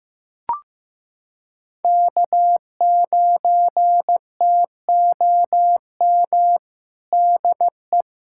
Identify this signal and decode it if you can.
Morse code, heard as a keyed tone, about 15 wpm, 700 Hz. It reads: K9TOM DE
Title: USA Callsign DE Callsign 15wpm